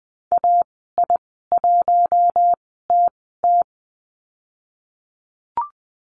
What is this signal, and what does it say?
Morse code, heard as a keyed tone, about 19 wpm, spaced as for 14 wpm, 700 Hz. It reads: AI1TT
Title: 055 Lesson 10g - Letter r - Call Signs 20wpm